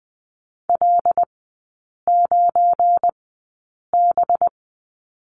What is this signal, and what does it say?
Morse code, heard as a keyed tone, about 20 wpm, spaced as for 8 wpm, 700 Hz. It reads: L9B
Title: 184 Lesson 30e - Letter b - Three Characters 20wpm